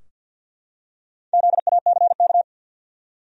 Morse code, heard as a keyed tone, about 53 wpm, 700 Hz. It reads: ZACK
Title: USA Common Names - Max 4 Letters - 50wpm